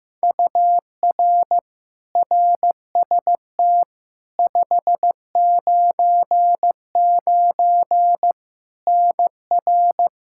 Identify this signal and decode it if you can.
Morse code, heard as a keyed tone, about 15 wpm, 700 Hz. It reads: UR RST 599 NR
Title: Practice QSO - Exchange 1 - Beginning - 15wpm